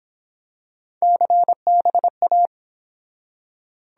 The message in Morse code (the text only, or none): CBA